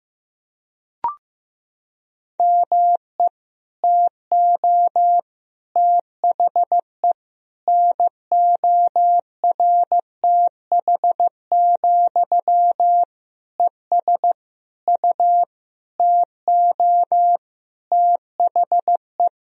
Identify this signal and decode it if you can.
Morse code, heard as a keyed tone, about 15 wpm, 700 Hz. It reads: ME TO THE NORTH, ES U TO THE
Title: Sentences from Top 400 Words and Abbreviations 15wpm